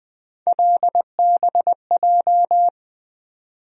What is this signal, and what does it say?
Morse code, heard as a keyed tone, about 20 wpm, 700 Hz. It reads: LBJ
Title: USA Call Sign Postfixes - 3 Letters 20wpm